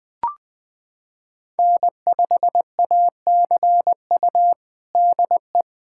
Morse code, heard as a keyed tone, about 21 wpm, 700 Hz. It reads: N5ACU DE
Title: USA Callsign DE Callsign - 5 Characters - 20wpm